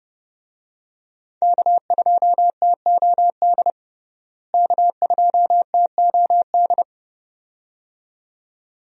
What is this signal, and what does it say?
Morse code, heard as a keyed tone, about 32 wpm, 700 Hz. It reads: K2TOD K2TOD
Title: USA Call Signs - 5 Characters - Repeated 2x - 30wpm